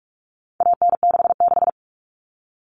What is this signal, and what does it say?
Morse code, heard as a keyed tone, about 44 wpm, 700 Hz. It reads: AN66
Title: All Cut Numbers - 4-Digit 45wpm